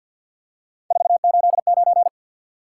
Morse code, heard as a keyed tone, about 53 wpm, 700 Hz. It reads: 489
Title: Top 180 QSO Elements 50wpm